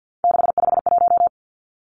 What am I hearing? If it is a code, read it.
Morse code, heard as a keyed tone, about 50 wpm, 700 Hz. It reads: 651
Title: Numbers - 3-Digit 50wpm